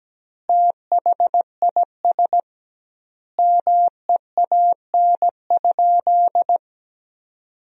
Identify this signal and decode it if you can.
Morse code, heard as a keyed tone, about 17 wpm, 700 Hz. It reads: THIS MEAN?
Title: 2x Word-Spacing - Sentences from Top 200 Words - 17wpm